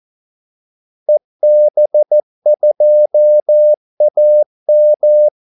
Morse code, heard as a keyed tone, about 14 wpm, 600 Hz. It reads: EB2AM
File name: Callsigns-14wpm